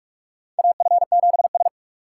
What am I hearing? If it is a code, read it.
Morse code, heard as a keyed tone, about 41 wpm, 700 Hz. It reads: AF7S
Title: USA Call Signs - 2x Character Spacing - 45wpm